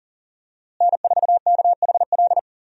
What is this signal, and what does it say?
Morse code, heard as a keyed tone, about 37 wpm, 700 Hz. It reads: N4KHL
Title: USA Call Signs - 5 Characters - Repeated 2x - 40wpm